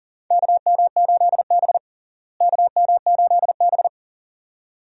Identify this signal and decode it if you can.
Morse code, heard as a keyed tone, about 39 wpm, 700 Hz. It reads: KM8B KM8B
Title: USA Call Signs - 4 Characters - Repeated 2x - 40wpm